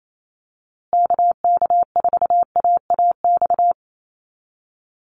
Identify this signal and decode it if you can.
Morse code, heard as a keyed tone, about 28 wpm, 700 Hz. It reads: KK4AAX
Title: USA Call Signs Repeated 3x 28wpm